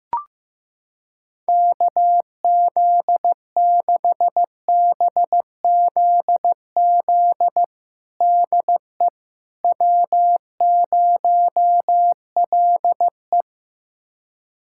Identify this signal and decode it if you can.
Morse code, heard as a keyed tone, about 15 wpm, 700 Hz. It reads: KZ6BZZ DE W0LE
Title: USA Callsign DE Callsign 15wpm